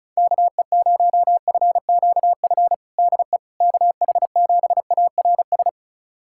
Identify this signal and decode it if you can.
Morse code, heard as a keyed tone, about 36 wpm, 700 Hz. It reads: KE0FQF DE KH7ARS
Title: USA Callsign DE Callsign - 6 Characters - 35wpm